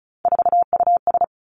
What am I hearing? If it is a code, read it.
Morse code, heard as a keyed tone, about 36 wpm, 700 Hz. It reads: 4US
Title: Three Challenging Letter-Number Combinations 35wpm